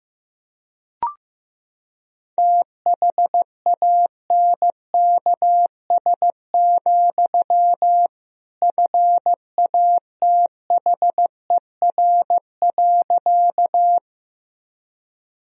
Morse code, heard as a keyed tone, about 15 wpm, 700 Hz. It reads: THANKS, FATHER.
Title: Sentences from Top 1000 Words - Review Part 3 - 15wpm